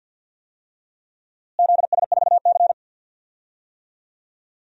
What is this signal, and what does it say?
Morse code, heard as a keyed tone, about 50 wpm, 700 Hz. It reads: ZS4C
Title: DXCC-100 Call Signs - 4 Characters - Repeated 3x - 50wpm